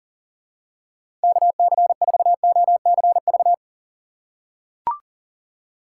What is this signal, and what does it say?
Morse code, heard as a keyed tone, about 40 wpm, 700 Hz. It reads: KC4OCV
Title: USA Call Signs - 6 Characters - 3x Character Spacing - 40wpm